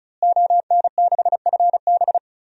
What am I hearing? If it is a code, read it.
Morse code, heard as a keyed tone, about 34 wpm, 700 Hz. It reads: ON6FB
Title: DXCC-100 Call Signs - 5 Characters - Repeated 3x - 35wpm